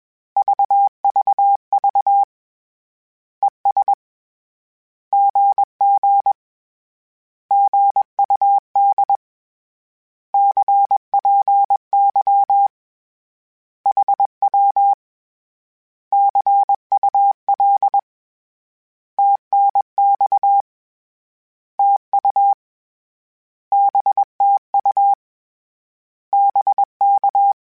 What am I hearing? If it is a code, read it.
Morse code, heard as a keyed tone, about 21 wpm, 800 Hz. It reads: VVV ES GG GUD CPY HW CUL TNX TU BTU BK
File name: Misc / Misc-mp3s / Words @20